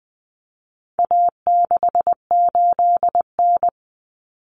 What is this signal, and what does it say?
Morse code, heard as a keyed tone, about 20 wpm, 700 Hz. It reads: A68N
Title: Common Cut Numbers - 4-Digit 20wpm